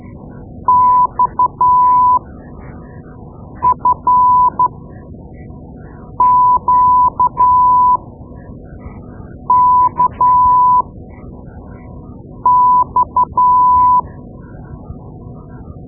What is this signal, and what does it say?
Morse code, heard as a keyed tone, about 12 wpm, spaced as for 5 wpm, 1000 Hz. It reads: XFQKX
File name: Morse Lessons / Morse Lessons / Tape-2B